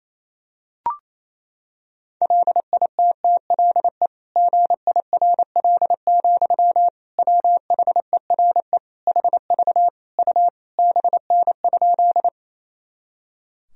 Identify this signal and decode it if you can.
Morse code, heard as a keyed tone, about 28 wpm, 700 Hz. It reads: LITTLE GIRL, WHERE HV U BN?
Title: Sentences from Top 400 Words and Abbreviations 28wpm